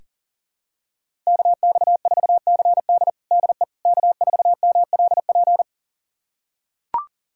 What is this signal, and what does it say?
Morse code, heard as a keyed tone, about 40 wpm, 700 Hz. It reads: KX4CD DE K4MLP
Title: USA Callsign DE Callsign - 5 Characters - 40wpm